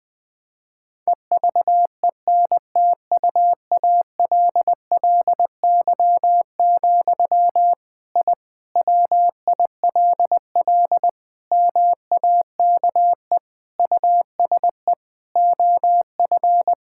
Morse code, heard as a keyed tone, about 21 wpm, 700 Hz. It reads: EVENTUALLY, I WILL MAKE USE OF
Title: Sentences from Top 1000 Words - Review Part 6 - 20wpm